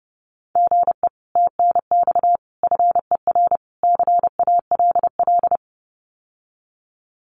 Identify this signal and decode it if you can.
Morse code, heard as a keyed tone, about 30 wpm, 700 Hz. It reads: GE TNX FER CALL
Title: Common QSO Phrases Encore 30wpm